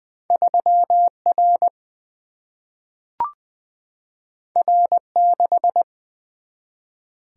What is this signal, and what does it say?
Morse code, heard as a keyed tone, about 20 wpm, 700 Hz. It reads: 3 R R 6
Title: Single-Letter Single-Number 20wpm